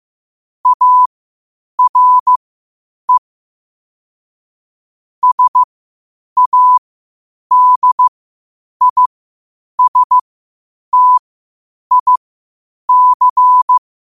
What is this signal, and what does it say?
Morse code, heard as a keyed tone, about 15 wpm, spaced as for 8 wpm, 1000 Hz. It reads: ARE SADISTIC
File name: qotdfiles / 8wpm